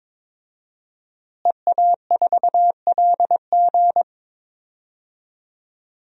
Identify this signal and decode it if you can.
Morse code, heard as a keyed tone, about 22 wpm, 700 Hz. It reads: EA4LG
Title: DXCC-100 Call Signs - 5 Characters - Repeated 3x - 22wpm